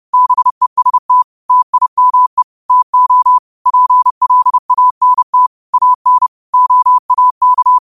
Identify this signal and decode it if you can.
Morse code, heard as a keyed tone, about 30 wpm, 1000 Hz. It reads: BEST TIME TO PLANT AN OAK
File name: qotdfiles / 30wpm